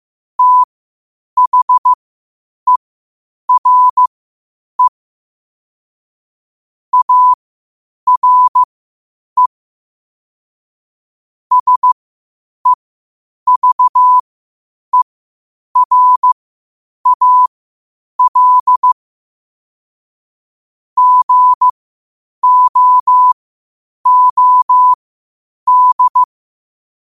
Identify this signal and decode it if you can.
Morse code, heard as a keyed tone, about 15 wpm, spaced as for 8 wpm, 1000 Hz. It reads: THERE ARE SEVERAL GOOD
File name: qotdfiles / 8wpm